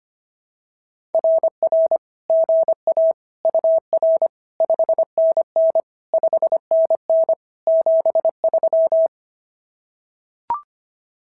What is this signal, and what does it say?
Morse code, heard as a keyed tone, about 25 wpm, 590 Hz. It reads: RR GA UR 5NN 5NN 73
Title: SOTA QSO Practice 25wpm